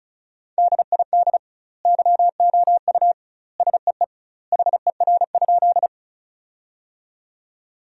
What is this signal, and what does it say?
Morse code, heard as a keyed tone, about 35 wpm, 700 Hz. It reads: DID YOU SEE HER?
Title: 2x Word-Spacing - Sentences from Top 100 Words - 35wpm